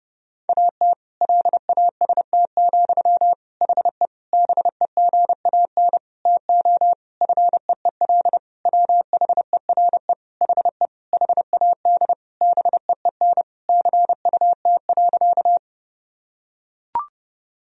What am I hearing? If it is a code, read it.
Morse code, heard as a keyed tone, about 32 wpm, 700 Hz. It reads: AT LAST, HE BEGAN TO FEEL WHERE HE HAD BEEN CUT.
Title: Sentences from Top 1000 Words - Review Part 1 - 30wpm